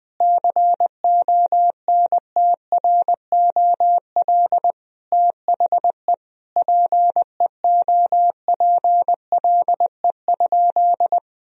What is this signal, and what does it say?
Morse code, heard as a keyed tone, about 21 wpm, 700 Hz. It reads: CONTROL THE PEOPLE?
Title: Sentences from Top 1000 Words - Review Part 3 - 20wpm